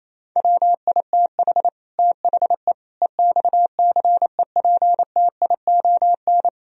Morse code, heard as a keyed tone, about 35 wpm, 700 Hz. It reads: WITH THE EXCEPTION
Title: Speed-Racing - Sets of 4 Words 28wpm